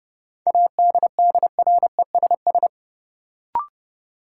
Morse code, heard as a keyed tone, about 30 wpm, 700 Hz. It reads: ADDRESS
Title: Sentences from Top 1000 Words - New Word Drill - 30wpm